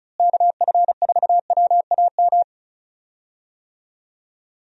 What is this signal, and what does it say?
Morse code, heard as a keyed tone, about 34 wpm, 700 Hz. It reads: KF4WAM
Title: USA Call Signs - 6 Characters - Repeated 2x - 35wpm